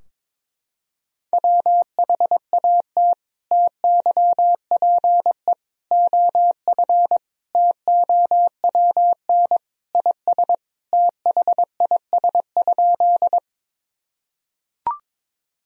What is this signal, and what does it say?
Morse code, heard as a keyed tone, about 22 wpm, 700 Hz. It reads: WHAT TYPE OF TOWN IS THIS?
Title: Sentences from Top 1000 Words - Review Part 3 - 22wpm